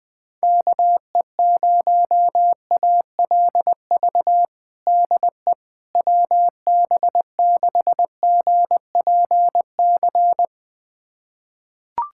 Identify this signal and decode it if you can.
Morse code, heard as a keyed tone, about 20 wpm, 700 Hz. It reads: KE0ALV DE WB6GPC
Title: USA Callsign DE Callsign - 6 Characters - 20wpm